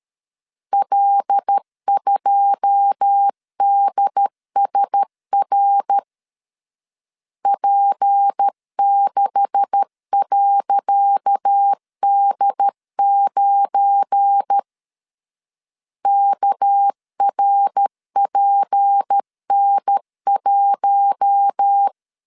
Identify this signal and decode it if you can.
Morse code, heard as a keyed tone, about 13 wpm, 800 Hz. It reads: L2DSR P6.D9 KRPN1
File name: LEZIONI / 10 SIGNS / 10C DOT 12wpm 2451